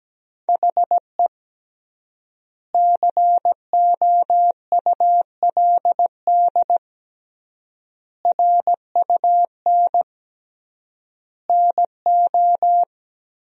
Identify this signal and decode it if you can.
Morse code, heard as a keyed tone, about 17 wpm, 700 Hz. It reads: HE COULD RUN NO
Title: 3x Word-Spacing - Sentences from Top 300 Words - 17wpm